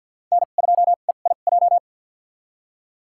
Morse code, heard as a keyed tone, about 48 wpm, spaced as for 35 wpm, 700 Hz. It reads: N2EIJ
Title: USA Call Signs - 5 Characters - 2x Character Spacing - 50wpm